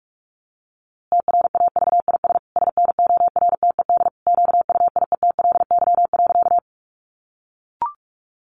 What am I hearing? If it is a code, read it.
Morse code, heard as a keyed tone, about 44 wpm, 700 Hz. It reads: TRAVIS SNORTED QUIETLY.
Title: A Sound Of Thunder 45wpm